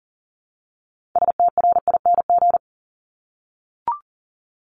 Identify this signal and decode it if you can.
Morse code, heard as a keyed tone, about 42 wpm, 700 Hz. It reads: STRING
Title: Multi-letter Phonemes 40wpm